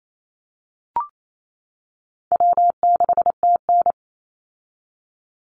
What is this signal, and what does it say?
Morse code, heard as a keyed tone, about 28 wpm, 700 Hz. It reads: W6TN
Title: Easy Call Signs - 28wpm